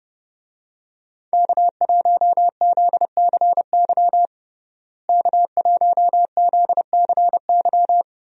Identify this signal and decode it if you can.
Morse code, heard as a keyed tone, about 30 wpm, 700 Hz. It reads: K1ZCY K1ZCY
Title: USA Call Signs - 5 Characters - Repeated 3x - 30wpm